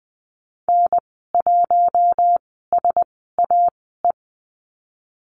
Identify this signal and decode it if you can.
Morse code, heard as a keyed tone, about 20 wpm, spaced as for 14 wpm, 700 Hz. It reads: N1SAE
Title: 042 Lesson 8g - Number 1 - Call Signs 20wpm